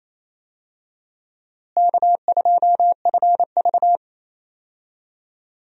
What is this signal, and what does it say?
Morse code, heard as a keyed tone, about 27 wpm, 700 Hz. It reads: K2FV
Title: USA Call Signs - 4 Characters - Repeated 3x - 28wpm